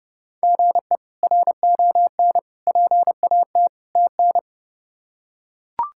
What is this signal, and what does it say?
Morse code, heard as a keyed tone, about 32 wpm, 700 Hz. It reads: GE RON PAT TN
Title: K1USN SST QSO Practice - 30wpm